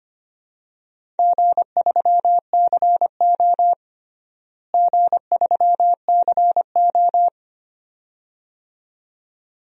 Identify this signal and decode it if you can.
Morse code, heard as a keyed tone, about 25 wpm, 700 Hz. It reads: G3CO G3CO
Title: DXCC-100 Call Signs - 4 Characters - Repeated 2x - 25wpm